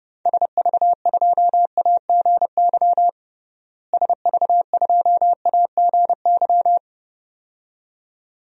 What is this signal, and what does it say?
Morse code, heard as a keyed tone, about 32 wpm, 700 Hz. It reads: SV2AGY SV2AGY
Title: DXCC-100 Call Signs - 6 Characters - Repeated 2x - 30wpm